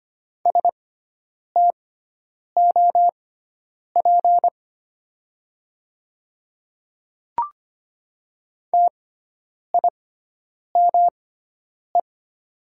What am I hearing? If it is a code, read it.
Morse code, heard as a keyed tone, about 25 wpm, spaced as for 9 wpm, 700 Hz. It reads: STOP TIME
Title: 6x Character Spacing - Top 300 Words - 25wpm